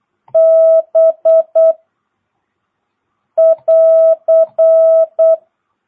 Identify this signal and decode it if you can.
Morse code, heard as a keyed tone, about 8 wpm, 660 Hz. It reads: B <AR>
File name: jh8cbh / 2022cw